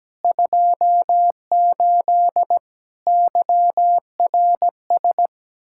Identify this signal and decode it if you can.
Morse code, heard as a keyed tone, about 17 wpm, 700 Hz. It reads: 28 YRS
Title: Practice QSO - Exchange 3 - More Info - 17wpm